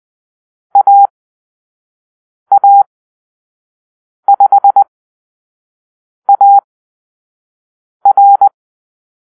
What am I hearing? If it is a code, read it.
Morse code, heard as a keyed tone, about 20 wpm, spaced as for 5 wpm, 800 Hz. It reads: AA5AR
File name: arrl / Call-Sign Drill-4